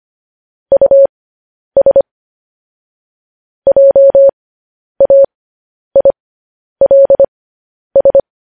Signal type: Morse code